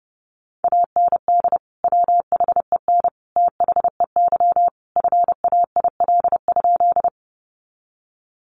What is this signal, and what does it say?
Morse code, heard as a keyed tone, about 30 wpm, 700 Hz. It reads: AND WHEN THEY FAIL?
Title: Sentences from Top 1100 Words 30wpm